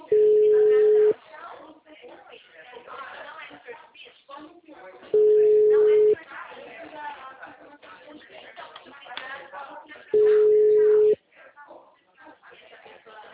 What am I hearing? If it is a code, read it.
Ringback tone (Europe, 425 Hz), with speech.